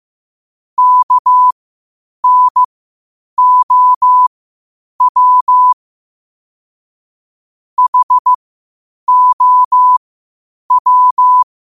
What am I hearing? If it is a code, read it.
Morse code, heard as a keyed tone, about 15 wpm, spaced as for 8 wpm, 1000 Hz. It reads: KNOW HOW
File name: qotdfiles / 8wpm